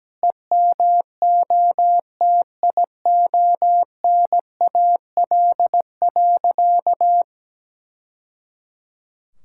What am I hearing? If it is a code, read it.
Morse code, heard as a keyed tone, about 17 wpm, 700 Hz. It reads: EMOTIONAL.
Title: Sentences from Top 1000 Words - Review Part 5 - 17wpm